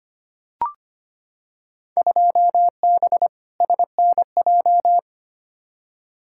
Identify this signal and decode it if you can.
Morse code, heard as a keyed tone, about 25 wpm, 700 Hz. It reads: 2B SNJ
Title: Field Day Exchanges Encore - 25wpm